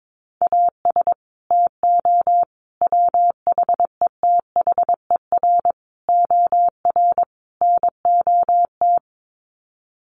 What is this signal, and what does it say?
Morse code, heard as a keyed tone, about 22 wpm, 700 Hz. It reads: AS TO WHETHER OR NOT
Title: Sets of 5 Words Encore 22wpm